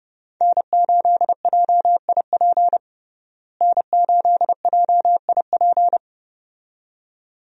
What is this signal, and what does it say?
Morse code, heard as a keyed tone, about 30 wpm, 700 Hz. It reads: N8JIP N8JIP
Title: USA Call Signs - 5 Characters - Repeated 2x - 30wpm